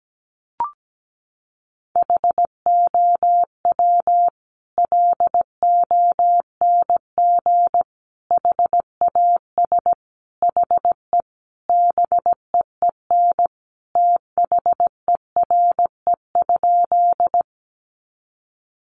Morse code, heard as a keyed tone, about 17 wpm, 700 Hz. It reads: HOW LONG HAS HE BEEN THERE?
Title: Sentences from Top 100 Words 17wpm